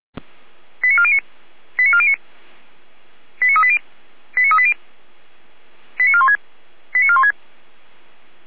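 Five-tone selective call (ZVEI), heard as quick runs of five tones, tones of about 70 ms. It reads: 89409 89409 89380 89380 89416 89416
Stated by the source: ZVEI five-tone selective call 89127 received on an SDR, from the feuerwehr-orlamuende.org (fire brigade Orlamünde) recording server.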